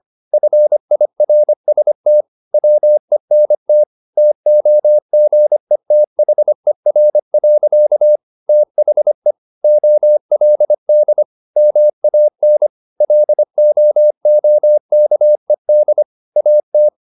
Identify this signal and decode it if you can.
Morse code, heard as a keyed tone, about 25 wpm, 600 Hz. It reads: FIRST WENT TOGETHER. THE OLD MAN LOOKED AT
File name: the old man and the sea part 1 25wpm